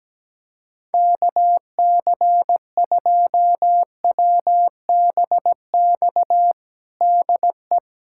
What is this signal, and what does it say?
Morse code, heard as a keyed tone, about 17 wpm, 700 Hz. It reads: KC2WBX DE
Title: USA Callsign DE Callsign - 6 Characters - 17wpm